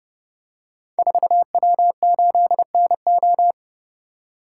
Morse code, heard as a keyed tone, about 30 wpm, 690 Hz. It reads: 4W8NO
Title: Code Groups - 5 Letter:Numbers - 30wpm